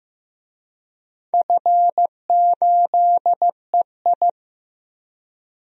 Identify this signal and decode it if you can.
Morse code, heard as a keyed tone, about 15 wpm, 700 Hz. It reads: F8EI
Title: DXCC-100 Call Signs - 4 Characters - 15wpm